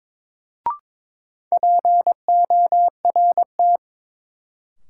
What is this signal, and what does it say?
Morse code, heard as a keyed tone, about 22 wpm, 700 Hz. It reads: PORT
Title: Sentences from Top 1000 Words - Review Part 1 - New Word Drill - 22wpm